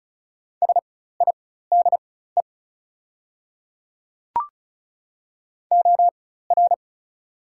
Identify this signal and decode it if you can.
Morse code, heard as a keyed tone, about 36 wpm, spaced as for 16 wpm, 690 Hz. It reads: SIDE OR
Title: 4x Character Spacing - Top 500 Words - 35wpm